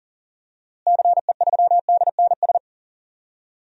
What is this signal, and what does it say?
Morse code, heard as a keyed tone, about 40 wpm, 700 Hz. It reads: CE3DNS
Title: DXCC-100 Call Signs - 6 Characters - Repeated 2x - 40wpm